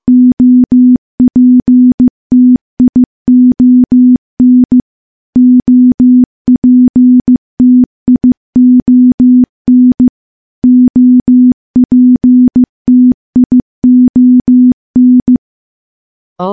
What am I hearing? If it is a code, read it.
Morse code, heard as a keyed tone, about 14 wpm, 250 Hz. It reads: OPTION OPTION OPTION
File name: morse-training-15WPM